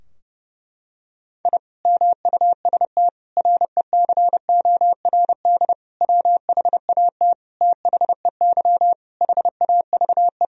Morse code, heard as a keyed tone, about 32 wpm, 700 Hz. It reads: I MUST RECORD WHAT THEY HAVE
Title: Sentences from Top 1000 Words - Review Part 3 - 30wpm